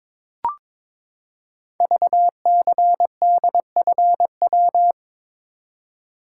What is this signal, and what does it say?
Morse code, heard as a keyed tone, about 22 wpm, 700 Hz. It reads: VCDFW
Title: Code Groups - 5 Letters - 22wpm